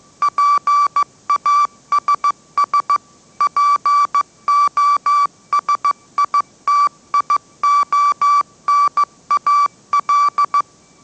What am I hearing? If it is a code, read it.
Morse code, heard as a keyed tone, about 15 wpm, 1160 Hz. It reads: PASS POSITIONAL
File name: mrx / mrx066 16wpm